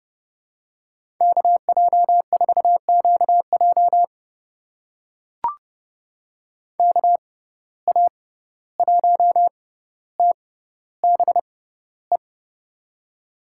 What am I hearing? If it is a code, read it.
Morse code, heard as a keyed tone, about 29 wpm, spaced as for 14 wpm, 700 Hz. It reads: KJ4QJ KA1TBE
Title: USA Call Signs - 6x Character Spacing - 30wpm